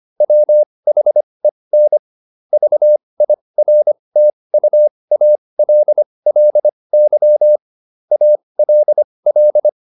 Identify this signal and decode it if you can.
Morse code, heard as a keyed tone, about 25 wpm, spaced as for 20 wpm, 600 Hz. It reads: WHEN VIRTUALLY ALL